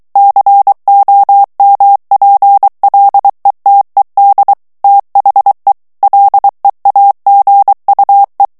Morse code, heard as a keyed tone, about 23 wpm, 800 Hz. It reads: COMPLETED THE LEAGUE
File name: ANGLAIS texte clair 3 vitesse 22 mots minute